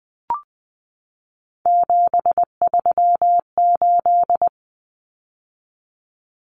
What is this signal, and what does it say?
Morse code, heard as a keyed tone, about 20 wpm, 700 Hz. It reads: 738
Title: Numbers - 3-Digit 20wpm